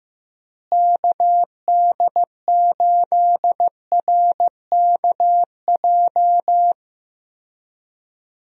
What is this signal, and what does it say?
Morse code, heard as a keyed tone, about 15 wpm, 700 Hz. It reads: KD8RKJ
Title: USA Call Signs - 6 Characters - Repeated 3x - 15wpm